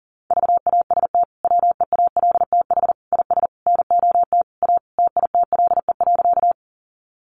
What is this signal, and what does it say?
Morse code, heard as a keyed tone, about 40 wpm, 700 Hz. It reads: VAST WEALTH IS NOT A TITLE.
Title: Sentences from Top 1300 Words 40wpm